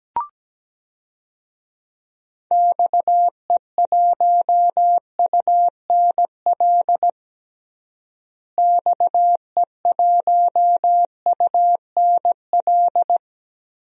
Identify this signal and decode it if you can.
Morse code, heard as a keyed tone, about 17 wpm, 700 Hz. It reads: XE1UNL XE1UNL
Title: DXCC-100 Call Signs - 6 Characters - Repeated 3x - 17wpm